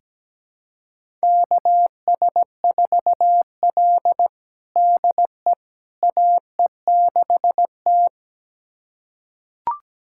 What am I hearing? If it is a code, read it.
Morse code, heard as a keyed tone, about 17 wpm, 700 Hz. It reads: KS4L DE AE6T
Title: USA Callsign DE Callsign - 4 Characters - 17wpm